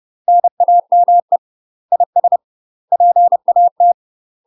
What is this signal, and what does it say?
Morse code, heard as a keyed tone, about 30 wpm, 700 Hz. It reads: NAME IS PAT
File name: UK MASTER QSOs TRAINING / 30WPM / UK MASTER QSOs 0050